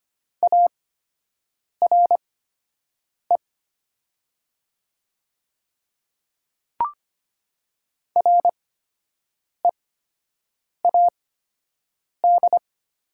Morse code, heard as a keyed tone, about 25 wpm, spaced as for 7 wpm, 700 Hz. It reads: ARE READ
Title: 8x Character Spacing - Top 200 Words - 25wpm